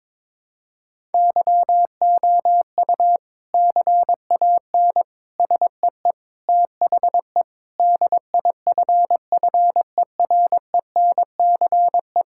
Morse code, heard as a keyed tone, about 22 wpm, 700 Hz. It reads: YOU CAN SEE THE DIFFERENCE
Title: Sentences from Top 1000 Words - Review Part 5 - 22wpm